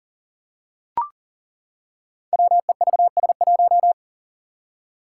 Morse code, heard as a keyed tone, about 40 wpm, 700 Hz. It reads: WEVS1
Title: Five Challenging Letter-Number Combinations 40wpm